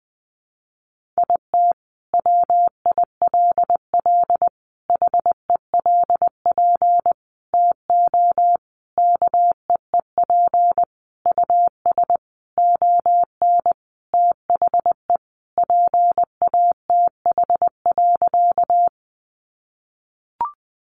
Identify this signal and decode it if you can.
Morse code, heard as a keyed tone, about 20 wpm, 700 Hz. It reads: IT WILL HELP TO KEEP US ON THE PATH.
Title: Sentences from Top 1400 Words 20wpm